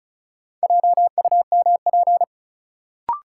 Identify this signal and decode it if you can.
Morse code, heard as a keyed tone, about 34 wpm, 700 Hz. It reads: JUMP
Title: Sentences from Top 1300 Words - New Word Drill Alt - 35wpm